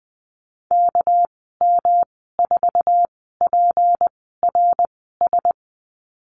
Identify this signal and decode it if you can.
Morse code, heard as a keyed tone, about 20 wpm, spaced as for 15 wpm, 700 Hz. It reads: KM4PRS
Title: 207 Lesson 33g - Letter j - Call Signs 20wpm